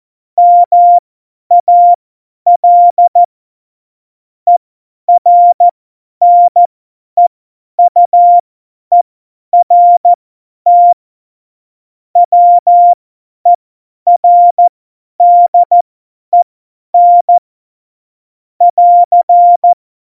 Morse code, heard as a keyed tone, about 14 wpm, spaced as for 10 wpm, 700 Hz. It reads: MAL ERNEUERT WERDEN <AR>
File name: Morsekurs / Witze / Witz 02. Januar 2018, 50 F70